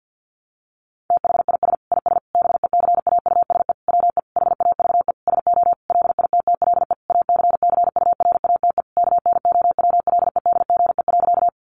Morse code, heard as a keyed tone, about 50 wpm, 700 Hz. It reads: THIS IS BECAUSE WE HAVE SO LITTLE ACCURATE KNOWLEDGE.
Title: Sentences from Top 1200 Words 50wpm